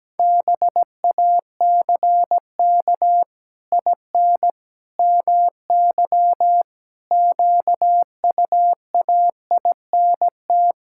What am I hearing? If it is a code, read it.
Morse code, heard as a keyed tone, about 17 wpm, 700 Hz. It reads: BACK IN MY QUAINT
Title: Pangram Challenge v2 - 17wpm